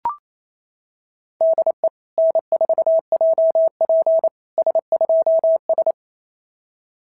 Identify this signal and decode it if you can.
Morse code, heard as a keyed tone, about 28 wpm, 660 Hz. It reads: DE N4JP S2S
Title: SOTA QSO Practice 28wpm